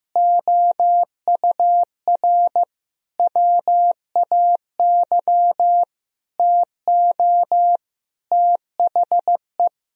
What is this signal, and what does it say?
Morse code, heard as a keyed tone, about 15 wpm, 700 Hz. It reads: OUR WAY TO THE
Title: Sentences from Top 1000 Words - Review Part 2 - 15wpm